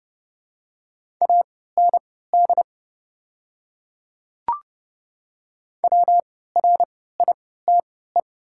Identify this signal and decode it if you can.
Morse code, heard as a keyed tone, about 31 wpm, spaced as for 17 wpm, 700 Hz. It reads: AND WRITE
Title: 3x Character Spacing - Top 100 Words - 30wpm